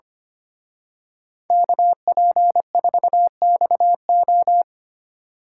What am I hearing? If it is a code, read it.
Morse code, heard as a keyed tone, about 30 wpm, 700 Hz. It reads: KP4XO
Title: Speed Racing - USA 2x2 Call Signs 25wpm